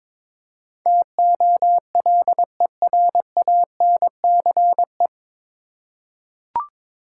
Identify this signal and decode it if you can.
Morse code, heard as a keyed tone, about 22 wpm, 700 Hz. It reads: TOLERANCE
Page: Morse Code Ninja practice 22wpm